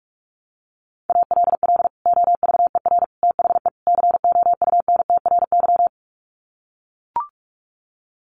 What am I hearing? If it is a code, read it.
Morse code, heard as a keyed tone, about 44 wpm, 700 Hz. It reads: ALL OVER THE COUNTRY
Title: Sets of 4 Words 45wpm